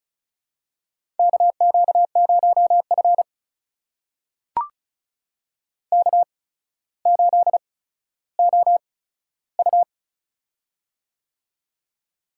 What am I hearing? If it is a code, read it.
Morse code, heard as a keyed tone, about 36 wpm, spaced as for 15 wpm, 700 Hz. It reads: KQ0F K8OU
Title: USA Call Signs - 4 Characters - 8x Character Spacing - 35wpm